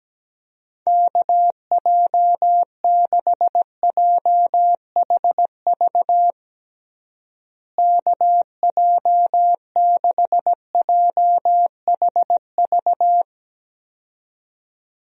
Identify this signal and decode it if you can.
Morse code, heard as a keyed tone, about 17 wpm, 700 Hz. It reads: KJ6JHV KJ6JHV
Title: USA Call Signs - 6 Characters - Repeated 2x - 17wpm